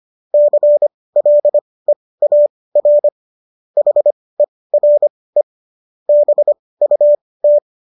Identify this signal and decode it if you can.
Morse code, heard as a keyed tone, about 25 wpm, spaced as for 18 wpm, 590 Hz. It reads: CLEAR HERE BUT
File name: qso223 18